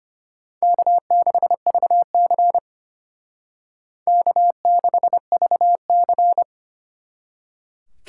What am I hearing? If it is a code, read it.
Morse code, heard as a keyed tone, about 31 wpm, 700 Hz. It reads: K6VC K6VC
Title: Speed Racing - USA 1x2 Call Signs 25wpm